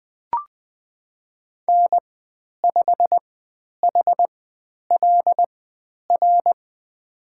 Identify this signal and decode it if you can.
Morse code, heard as a keyed tone, about 20 wpm, spaced as for 10 wpm, 700 Hz. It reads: N5HLR
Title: 095 Lesson 16g - Letter c - Call Signs 20wpm